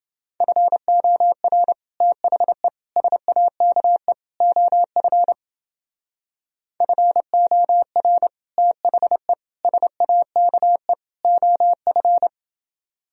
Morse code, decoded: FOR THE SAKE OF FOR THE SAKE OF

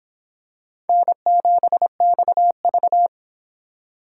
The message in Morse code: N7XV

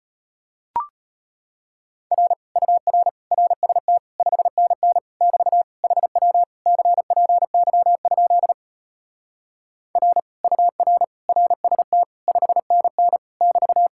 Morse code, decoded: R UR RST 5NN <BT> HW CPY? R UR RST 5NN <BT>

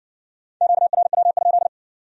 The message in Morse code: QRW?

